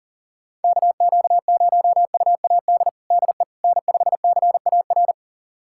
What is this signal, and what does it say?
Morse code, heard as a keyed tone, about 40 wpm, 700 Hz. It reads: KQ0UAD DE N5CAR
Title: USA Callsign DE Callsign 40wpm